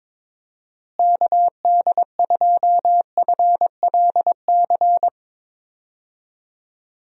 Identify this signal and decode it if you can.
Morse code, heard as a keyed tone, about 22 wpm, 700 Hz. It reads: KD2FLC
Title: USA Call Signs - 6 Characters - Repeated 2x - 22wpm